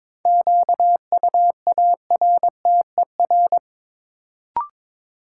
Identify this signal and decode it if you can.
Morse code, heard as a keyed tone, about 22 wpm, 700 Hz. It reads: QUARTER
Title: Sentences from Top 1300 Words - New Word Drill Alt - 22wpm